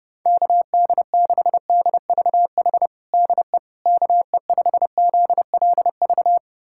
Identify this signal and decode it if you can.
Morse code, heard as a keyed tone, about 30 wpm, 700 Hz. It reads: KD6DVH DE KE5ZLV
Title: USA Callsign DE Callsign 30wpm